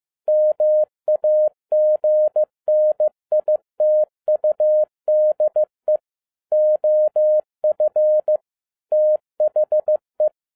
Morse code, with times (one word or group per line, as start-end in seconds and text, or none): 0.3-6.0 MAGNITUDE
6.5-8.4 OF
8.9-10.3 THE